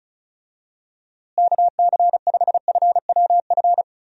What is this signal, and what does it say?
Morse code, heard as a keyed tone, about 36 wpm, 700 Hz. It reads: KC5FWF
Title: USA Call Signs - 6 Characters - Repeated 2x - 35wpm